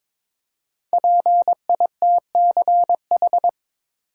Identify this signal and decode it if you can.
Morse code, heard as a keyed tone, about 22 wpm, 700 Hz. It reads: PITCH
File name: 5 Letter Words - First 3 Letters Given - 22wpm